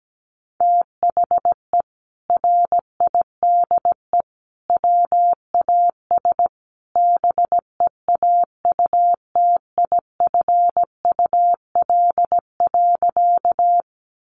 Morse code, heard as a keyed tone, about 17 wpm, 700 Hz. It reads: THE RIDE WAS BEAUTIFUL.
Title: Sentences from Top 1200 Words 17wpm